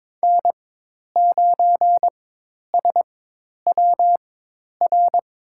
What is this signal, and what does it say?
Morse code, heard as a keyed tone, about 21 wpm, spaced as for 10 wpm, 700 Hz. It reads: N9SWR
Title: USA Call Signs - 5 Characters - 4x Character Spacing - 22wpm